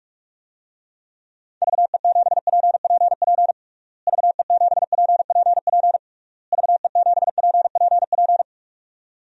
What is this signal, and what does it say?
Morse code, heard as a keyed tone, about 42 wpm, 700 Hz. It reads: VE7PPP VE7PPP VE7PPP
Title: DXCC-100 Call Signs - 6 Characters - Repeated 3x - 45wpm